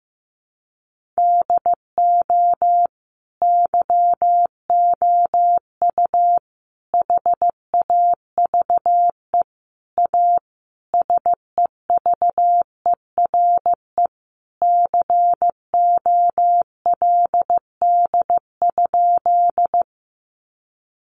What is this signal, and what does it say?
Morse code, heard as a keyed tone, about 15 wpm, 700 Hz. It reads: DO YOU HAVE A SEVERE COLD?
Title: Sentences from Top 1400 Words 15wpm